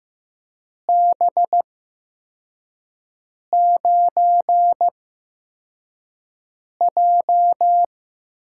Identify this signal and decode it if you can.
Morse code, heard as a keyed tone, about 15 wpm, spaced as for 4 wpm, 700 Hz. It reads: B9J
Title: USA Call Signs - 8x Character Spacing - 15wpm